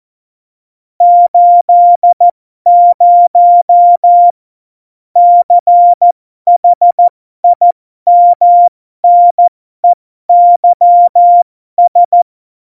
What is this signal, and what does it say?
Morse code, heard as a keyed tone, about 14 wpm, spaced as for 12 wpm, 700 Hz. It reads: 80 CHIMNEYS
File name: Morsekurs / Witze